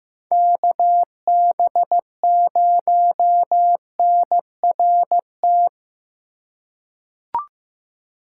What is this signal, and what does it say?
Morse code, heard as a keyed tone, about 15 wpm, 700 Hz. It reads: KB0NRT
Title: USA Call Signs - 6 Characters - 3x Character Spacing - 15wpm